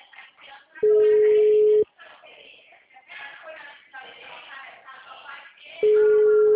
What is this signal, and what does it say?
Ringback tone (Europe, 425 Hz), with speech.
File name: out- -2010- - - .61663